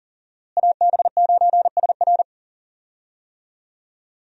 Morse code, heard as a keyed tone, about 39 wpm, 700 Hz. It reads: AB9SR
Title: USA Call Signs - 5 Characters - Repeated 2x - 40wpm